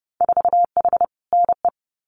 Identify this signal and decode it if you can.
Morse code, heard as a keyed tone, about 30 wpm, 700 Hz. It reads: 4H NE
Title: Winter Field Day Exchanges Encore 30wpm